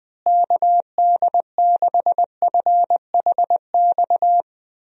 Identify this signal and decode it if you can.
Morse code, heard as a keyed tone, about 21 wpm, 700 Hz. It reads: KD6FHX
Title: USA Call Signs - 6 Characters - Repeated 2x - 20wpm